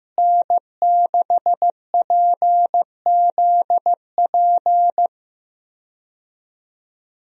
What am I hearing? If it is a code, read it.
Morse code, heard as a keyed tone, about 16 wpm, 700 Hz. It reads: N6PZP
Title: USA Call Signs - 5 Characters - Repeated 2x - 15wpm